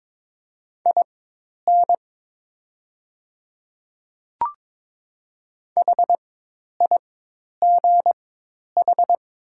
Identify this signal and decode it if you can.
Morse code, heard as a keyed tone, about 22 wpm, spaced as for 10 wpm, 700 Hz. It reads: IN HIGH
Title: 4x Character Spacing - Top 200 Words - 22wpm